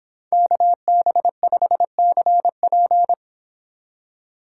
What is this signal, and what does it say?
Morse code, heard as a keyed tone, about 28 wpm, 700 Hz. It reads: KB5CP